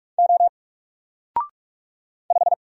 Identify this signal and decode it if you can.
Morse code, heard as a keyed tone, about 46 wpm, 700 Hz. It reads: O 5
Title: Single Letter-Number Alt 45wpm